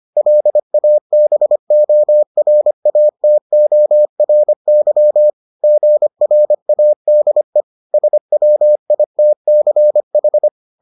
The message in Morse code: LABORATORY GRADE SWITCH

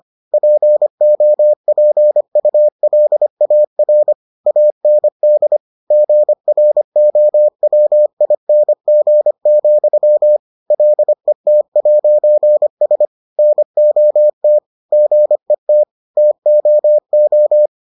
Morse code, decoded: POPULAR AND GROWING, LET'S NOT GET TOO

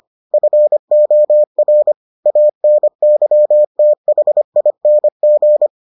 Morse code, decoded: FOR ANYTHING

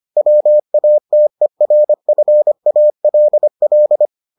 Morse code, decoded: WATERFALL

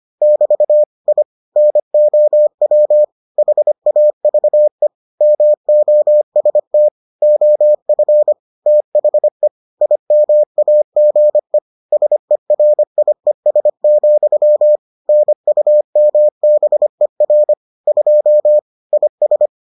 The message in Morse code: <BT> I NOW HAVE MOST OF THE IMAGE SERIES, NUMBER 2 IS

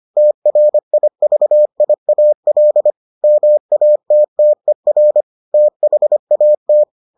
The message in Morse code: TRIVIAL MATTER THAT

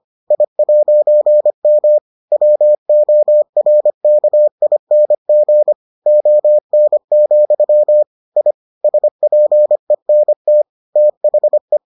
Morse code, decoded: I'M WORKING ON, I SPENT THE

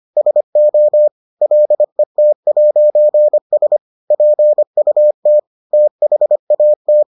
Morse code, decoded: SO LET'S PUT THAT

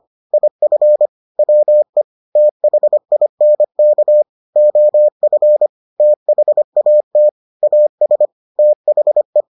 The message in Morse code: IF WE THINK OF THAT AS THE

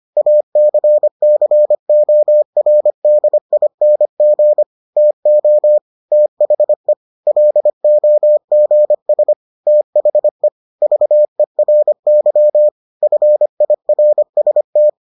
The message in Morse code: ACCORDING TO THE LOGS THE VERY FIRST

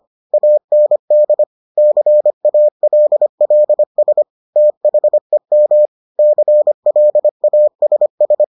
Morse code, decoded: AND CALLS THEM CLASS

